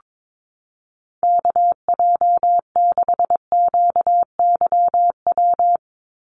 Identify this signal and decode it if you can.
Morse code, heard as a keyed tone, about 22 wpm, 700 Hz. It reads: KJ6QYW
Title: USA Call Signs 22wpm